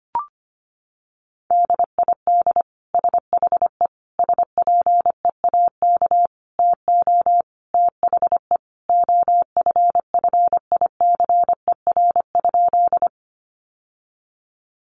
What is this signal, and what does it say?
Morse code, heard as a keyed tone, about 25 wpm, 700 Hz. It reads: DID SHE SPEAK TO THE OFFICER?
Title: Sentences from Top 1200 Words 25wpm